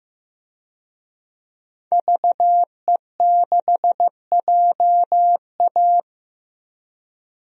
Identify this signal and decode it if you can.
Morse code, heard as a keyed tone, about 15 wpm, 700 Hz. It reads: VE6JA
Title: DXCC-100 Call Signs - 5 Characters - 15wpm